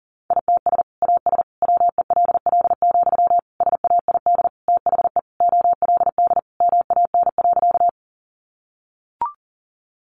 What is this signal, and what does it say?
Morse code, heard as a keyed tone, about 40 wpm, 700 Hz. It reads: ITS AS WELL, SAID THE OLD MAN.
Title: Sentences from Top 600 Words 40wpm